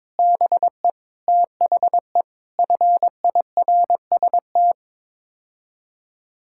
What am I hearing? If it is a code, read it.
Morse code, heard as a keyed tone, about 22 wpm, 700 Hz. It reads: BE THE FIRST
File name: practice / 22wpm / Sets of 6 Words - 22wpm